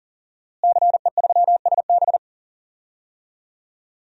Morse code, heard as a keyed tone, about 40 wpm, 700 Hz. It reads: CE3SB
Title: DXCC-100 Call Signs - 5 Characters - Repeated 2x - 40wpm